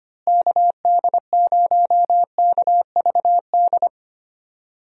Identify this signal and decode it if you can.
Morse code, heard as a keyed tone, about 25 wpm, 700 Hz. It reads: KD0KVD
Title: USA Call Signs - 6 Characters - Repeated 2x - 25wpm